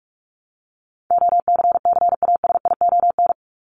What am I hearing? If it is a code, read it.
Morse code, heard as a keyed tone, about 44 wpm, 700 Hz. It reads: OCCASION
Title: Top 1501-1750 Words 45wpm